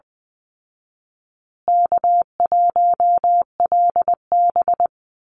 Morse code, heard as a keyed tone, about 20 wpm, 700 Hz. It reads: K1LB
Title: Easy Call Signs Repeated 3x - 20wpm